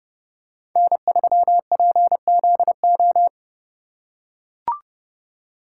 Morse code, heard as a keyed tone, about 29 wpm, 700 Hz. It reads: N3PZO